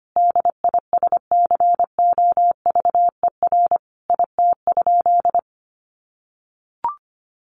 Morse code, heard as a keyed tone, about 25 wpm, 700 Hz. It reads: DISCOVER IT?
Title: Sentences from Top 1100 Words 25wpm